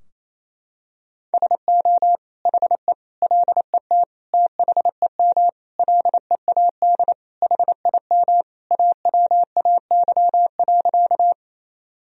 Morse code, heard as a keyed tone, about 28 wpm, 700 Hz. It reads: SO HE LET THEM LEAD HIM AWAY.
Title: Sentences from Top 1000 Words - Review Part 2 - 28wpm